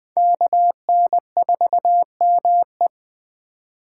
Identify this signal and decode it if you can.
Morse code, heard as a keyed tone, about 20 wpm, 700 Hz. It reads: KN4ME
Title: Speed Racing - USA 2x2 Call Signs 17wpm